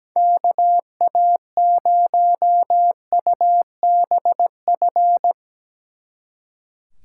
Morse code, heard as a keyed tone, about 18 wpm, 700 Hz. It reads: KA0UBF
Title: Speed Racing - USA 2x3 Call Signs 17wpm